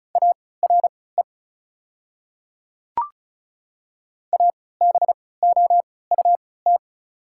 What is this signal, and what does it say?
Morse code, heard as a keyed tone, about 36 wpm, spaced as for 20 wpm, 690 Hz. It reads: ARE ABOUT
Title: 3x Character Spacing - Top 200 Words - 35wpm